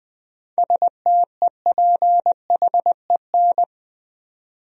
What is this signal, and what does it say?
Morse code, heard as a keyed tone, about 20 wpm, 700 Hz. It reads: STEPHEN
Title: USA Common Names - 20wpm